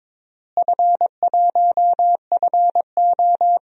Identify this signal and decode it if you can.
Morse code, heard as a keyed tone, about 23 wpm, 700 Hz. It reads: F1FO
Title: DXCC-100 Call Signs - 4 Characters - Repeated 3x - 22wpm